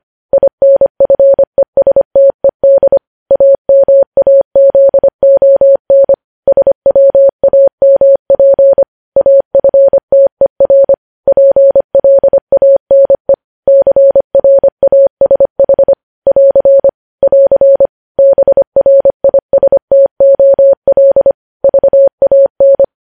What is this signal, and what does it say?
Morse code, heard as a keyed tone, about 25 wpm, 550 Hz. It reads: INFESTED AMAZON SWAMP AFTER PLANE CRASH <AR> <AR> BRISTOL VAN
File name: podcast / News headlines-1x-25x25